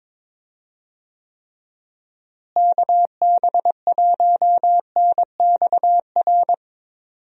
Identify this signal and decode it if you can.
Morse code, heard as a keyed tone, about 22 wpm, 700 Hz. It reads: KB1NXR
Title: USA Call Signs - 6 Characters - Repeated 3x - 22wpm